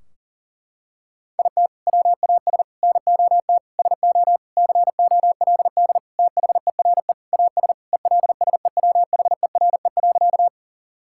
Morse code, heard as a keyed tone, about 40 wpm, 700 Hz. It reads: IT WAS NOT SO COLD THERE AS ELSEWHERE.
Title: Sentences from Top 1600 Words 40wpm